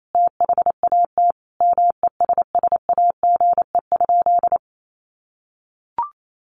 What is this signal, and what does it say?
Morse code, heard as a keyed tone, about 28 wpm, 700 Hz. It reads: THAT MESSAGE?
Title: Sentences from Top 1100 Words 28wpm